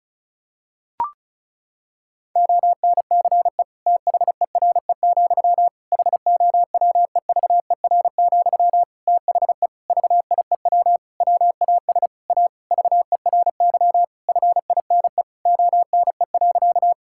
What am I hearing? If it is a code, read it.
Morse code, heard as a keyed tone, about 36 wpm, 700 Hz. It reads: ONCE THERE, HOWEVER, THE VIEW WAS A VERY FINE ONE.
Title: Sentences from Top 1000 Words - Review Part 6 - 35wpm